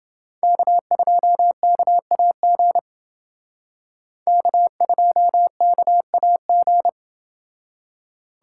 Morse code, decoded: K2KAG K2KAG

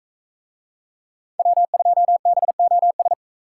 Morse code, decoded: W2BOS

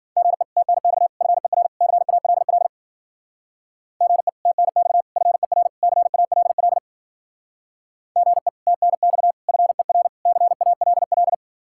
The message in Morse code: GE TNX FER CALL GE TNX FER CALL GE TNX FER CALL

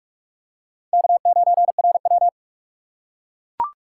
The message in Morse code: K9RW

